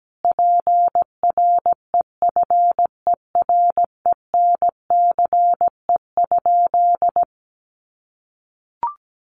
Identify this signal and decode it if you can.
Morse code, heard as a keyed tone, about 17 wpm, 700 Hz. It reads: PREFERENCE?
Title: Sentences from Top 1500 Words 17wpm